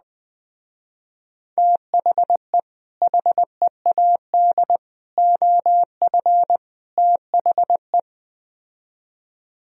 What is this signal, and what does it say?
Morse code, heard as a keyed tone, about 20 wpm, 700 Hz. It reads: THE HEAD OF THE
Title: Speed-Racing - Sets of 4 Words 17wpm